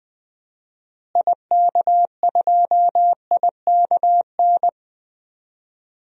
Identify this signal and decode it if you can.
Morse code, heard as a keyed tone, about 20 wpm, 700 Hz. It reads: IK2IKN
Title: DXCC-100 Call Signs - 6 Characters - Repeated 3x - 20wpm